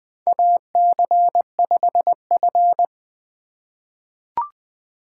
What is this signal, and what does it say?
Morse code, heard as a keyed tone, about 21 wpm, 700 Hz. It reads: AC5F
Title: USA Call Signs - 2x Character Spacing - 20wpm